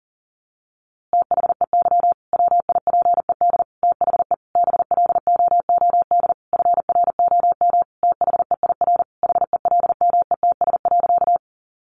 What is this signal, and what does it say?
Morse code, heard as a keyed tone, about 40 wpm, 700 Hz. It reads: THEY WIPED THE BLOOD FROM THEIR HELMETS.
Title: A Sound Of Thunder 40wpm